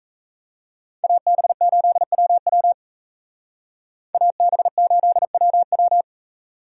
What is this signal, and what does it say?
Morse code, heard as a keyed tone, about 34 wpm, 700 Hz. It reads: AB8WW AB8WW
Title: Speed Racing - USA 2x2 Call Signs 28wpm